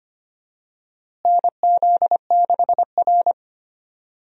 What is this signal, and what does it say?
Morse code, heard as a keyed tone, about 25 wpm, 700 Hz. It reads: NZ6R
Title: USA Call Signs - 4 Characters - Repeated 2x - 25wpm